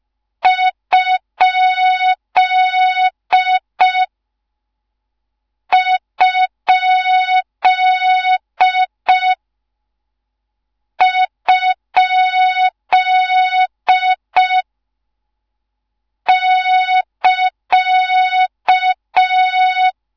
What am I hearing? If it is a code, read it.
Morse code, heard as a keyed tone, about 5 wpm, 3700 Hz, with noise. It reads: ? ? ? <KA>
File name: Morze-HA2MN / hang / 13IRASJELEK1 HA2MN